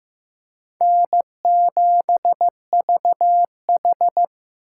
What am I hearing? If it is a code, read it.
Morse code, heard as a keyed tone, about 15 wpm, 700 Hz. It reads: N7VH
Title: USA Call Signs - 4 Characters - 4x Character Spacing - 15wpm